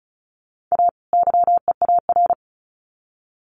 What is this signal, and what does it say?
Morse code, heard as a keyed tone, about 34 wpm, 700 Hz. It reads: A YEAR
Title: Sets of 2 Words 35wpm